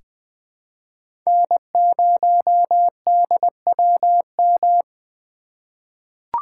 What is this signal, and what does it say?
Morse code, heard as a keyed tone, about 20 wpm, 700 Hz. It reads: N0DWM
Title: USA Call Signs - 5 Characters - 4x Character Spacing - 20wpm